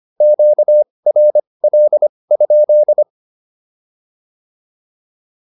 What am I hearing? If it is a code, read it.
Morse code, heard as a keyed tone, about 25 wpm, spaced as for 20 wpm, 600 Hz. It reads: QRL?
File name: QSO102 20